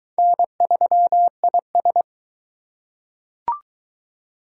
Morse code, heard as a keyed tone, about 20 wpm, 700 Hz. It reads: N3IS